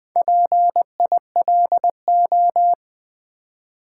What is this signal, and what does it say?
Morse code, heard as a keyed tone, about 20 wpm, 700 Hz. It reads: PILO
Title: Build A Word - 1 Letter At A Time - 5-Letter Words - 20wpm